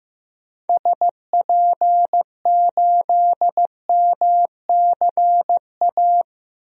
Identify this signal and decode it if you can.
Morse code, heard as a keyed tone, about 15 wpm, 700 Hz. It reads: SP8MCA
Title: DXCC-100 Call Signs - 6 Characters - Repeated 3x - 15wpm